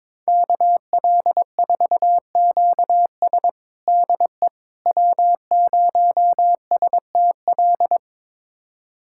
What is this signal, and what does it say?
Morse code, heard as a keyed tone, about 22 wpm, 700 Hz. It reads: KL4QS DE W0STL
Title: USA Callsign DE Callsign - 5 Characters - 22wpm